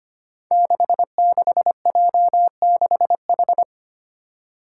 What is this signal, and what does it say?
Morse code, heard as a keyed tone, about 25 wpm, 700 Hz. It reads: 66J6H
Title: Five Challenging Letter-Number Combinations 25wpm